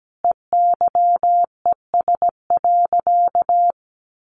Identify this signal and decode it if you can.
Morse code, heard as a keyed tone, about 17 wpm, 700 Hz. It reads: EYES.
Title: Easy Phrases 17wpm